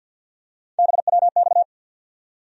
Morse code, heard as a keyed tone, about 52 wpm, 690 Hz. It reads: BWX
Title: USA Call Sign Postfixes - 3 Letters 50wpm